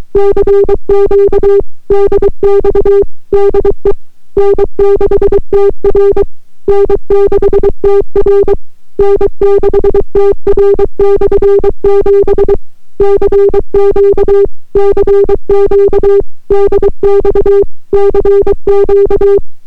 Morse code, heard as a keyed tone, about 23 wpm, 400 Hz. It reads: CQ DX DE N6TR N6TR N6TR/7 CQ CQ DX CQ